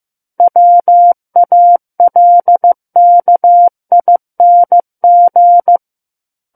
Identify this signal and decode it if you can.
Morse code, heard as a keyed tone, about 15 wpm, 700 Hz. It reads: WALKING
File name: Book One / 06-Chapter 6 - 15 WPM